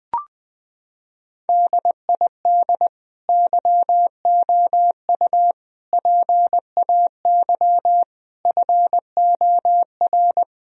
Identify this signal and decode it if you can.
Morse code, heard as a keyed tone, about 20 wpm, 700 Hz. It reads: DID YOU PAY FOR
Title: Sentences from Top 1000 Words - Review Part 3 - 20wpm